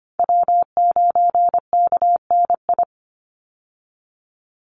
Morse code, heard as a keyed tone, about 25 wpm, 700 Hz. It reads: W9KNI
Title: Call Signs - 25wpm